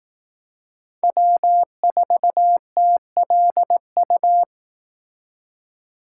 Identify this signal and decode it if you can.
Morse code, heard as a keyed tone, about 18 wpm, 700 Hz. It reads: W4TLU
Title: Speed Racing - USA 1x3 Call Signs 15wpm